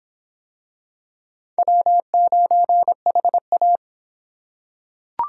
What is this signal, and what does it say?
Morse code, heard as a keyed tone, about 26 wpm, 700 Hz. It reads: W9HA